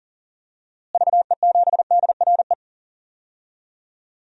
Morse code, heard as a keyed tone, about 40 wpm, 700 Hz. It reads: VE7DRE
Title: DXCC-100 Call Signs - 6 Characters - Repeated 2x - 40wpm